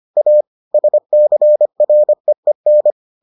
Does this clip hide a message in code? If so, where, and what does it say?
Yes: Morse code from 0.2 to 2.9 s, heard as a keyed tone, about 25 wpm, 600 Hz. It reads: A SCREEN